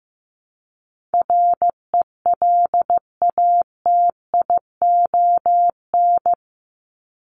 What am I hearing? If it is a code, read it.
Morse code, heard as a keyed tone, about 15 wpm, 700 Hz. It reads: RELATION
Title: ES:S Suffix - Top 100 Words with Base Words 15wpm